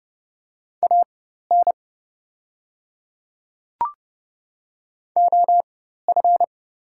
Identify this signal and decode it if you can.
Morse code, heard as a keyed tone, about 31 wpm, spaced as for 14 wpm, 700 Hz. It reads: AN OF
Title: 4x Character Spacing - Top 100 Words - 30wpm